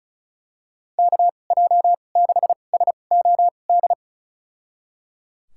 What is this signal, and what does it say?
Morse code, heard as a keyed tone, about 36 wpm, spaced as for 25 wpm, 700 Hz. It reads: KJ6SOD
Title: USA Call Signs - 6 Characters - 2x Character Spacing - 35wpm